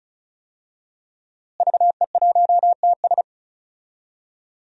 Morse code, heard as a keyed tone, about 36 wpm, 700 Hz. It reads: VE1TS
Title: DXCC-100 Call Signs - 5 Characters - Repeated 3x - 35wpm